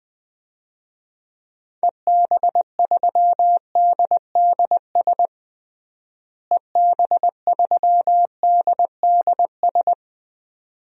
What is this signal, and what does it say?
Morse code, heard as a keyed tone, about 21 wpm, 700 Hz. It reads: EB3DDS EB3DDS
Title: DXCC-100 Call Signs - 6 Characters - Repeated 2x - 20wpm